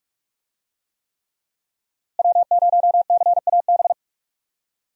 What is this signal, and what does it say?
Morse code, heard as a keyed tone, about 42 wpm, 700 Hz. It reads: W0CAB
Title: USA Call Signs - 5 Characters - Repeated 3x - 45wpm